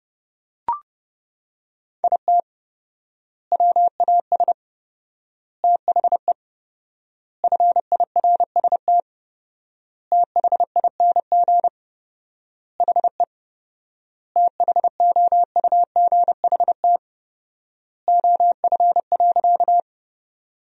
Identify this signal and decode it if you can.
Morse code, heard as a keyed tone, about 32 wpm, 700 Hz. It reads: IT WAS THE FIRST THING HE THOUGHT OF.
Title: 4x Word-Spacing - Sentences from Top 300 Words - 30wpm